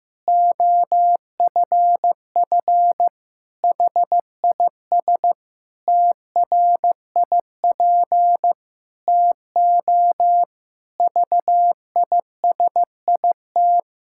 Morse code, heard as a keyed tone, about 15 wpm, 700 Hz. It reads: OFF HIS TRIP TO VISIT
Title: Pangram Challenge v2 - 15wpm